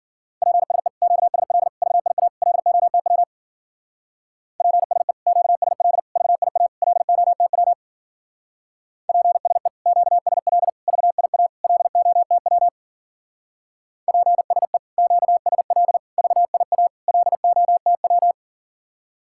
Morse code, heard as a keyed tone, about 47 wpm, 700 Hz. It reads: PSE QSL VIA LOTW PSE QSL VIA LOTW PSE QSL VIA LOTW PSE QSL VIA LOTW
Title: Speed-Racing Common QSO Phrases Encore - 40wpm